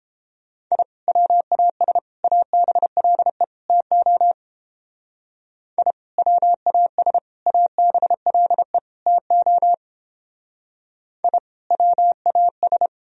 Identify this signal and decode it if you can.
Morse code, heard as a keyed tone, about 31 wpm, 700 Hz. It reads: I WAS ABLE TO I WAS ABLE TO I WAS